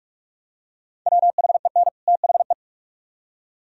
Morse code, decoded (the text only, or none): WHEN THE